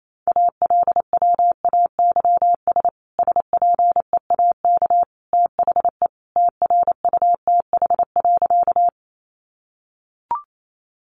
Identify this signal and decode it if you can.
Morse code, heard as a keyed tone, about 28 wpm, 700 Hz. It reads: ALWAYS SPEAK THE TRUTH.
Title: Sentences from Top 1100 Words 28wpm